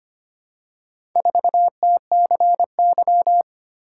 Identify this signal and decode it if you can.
Morse code, heard as a keyed tone, about 25 wpm, 700 Hz. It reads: 4TCY
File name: practice / 25wpm / Four Letter-Number Combinations - 25wpm